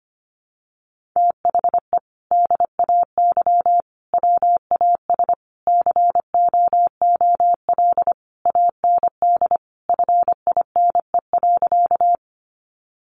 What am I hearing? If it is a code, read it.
Morse code, heard as a keyed tone, about 25 wpm, 700 Hz. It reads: THE DAY WAS COOL AND FINE.
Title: Sentences from Top 1200 Words 25wpm